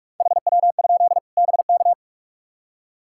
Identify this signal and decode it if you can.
Morse code, heard as a keyed tone, about 46 wpm, 700 Hz. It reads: HW? BK
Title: Practice QSO - Exchange 2 - Info - 45wpm